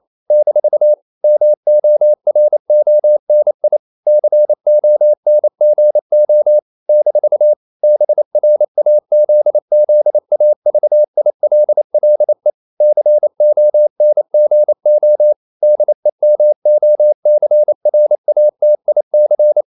Morse code, heard as a keyed tone, about 28 wpm, 600 Hz. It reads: - MORONI CONGO - BRAZZAVILLE CONGO DEMOCRATIC
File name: CW / Practice / NationalCapitals28